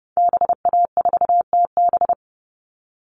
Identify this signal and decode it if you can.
Morse code, heard as a keyed tone, about 30 wpm, 700 Hz. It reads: BA4TB
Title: Call Signs Repeated 2x - 30wpm